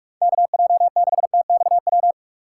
Morse code, heard as a keyed tone, about 46 wpm, 700 Hz. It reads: KJ6TXW
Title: USA Call Signs - 6 Characters - 6x Character Spacing - 45wpm